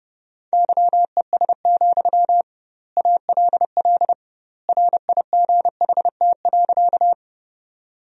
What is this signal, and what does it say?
Morse code, heard as a keyed tone, about 30 wpm, 700 Hz. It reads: YES, ALL RIGHT.
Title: 2x Word-Spacing - Sentences from Top 500 Words - 30wpm